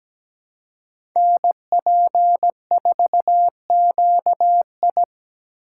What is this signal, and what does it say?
Morse code, heard as a keyed tone, about 17 wpm, 700 Hz. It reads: NP4QI
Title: USA Call Signs - 2x Character Spacing - 17wpm